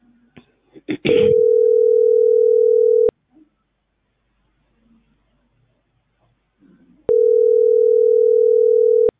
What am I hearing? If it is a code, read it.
Ringback tone (North America), 440 + 480 Hz.